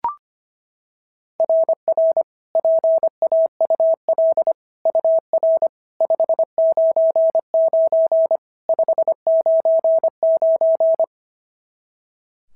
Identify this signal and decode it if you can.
Morse code, heard as a keyed tone, about 23 wpm, 650 Hz. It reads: RR PAUL UR 599 599
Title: SOTA QSO Practice 25wpm